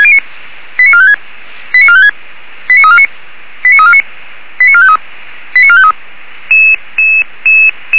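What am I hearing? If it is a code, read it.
Five-tone selective call (ZVEI), heard as quick runs of five tones, tones of about 70 ms. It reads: 89456 89456 89349 89349 89453 89453
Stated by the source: Five-tone ZVEI selective call 89350, received on an SDR.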